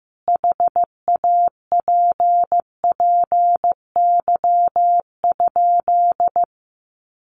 Morse code, heard as a keyed tone, about 15 wpm, 700 Hz. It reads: HAPPY?
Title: Sentences from Top 1000 Words 15wpm